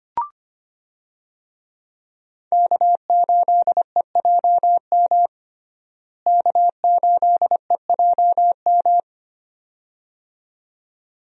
Morse code, heard as a keyed tone, about 25 wpm, 700 Hz. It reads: K8EJM K8EJM
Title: USA Call Signs - 5 Characters - Repeated 2x - 25wpm